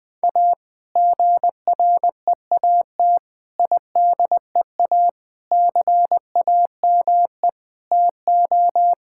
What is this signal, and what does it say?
Morse code, heard as a keyed tone, about 21 wpm, 700 Hz. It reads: A GREAT IDEA CAME TO
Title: Sentences with 2-Word Phrases - 20wpm